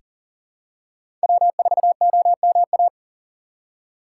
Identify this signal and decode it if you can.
Morse code, heard as a keyed tone, about 40 wpm, 700 Hz. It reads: W4OMA
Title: USA Call Signs - 5 Characters - 6x Character Spacing - 40wpm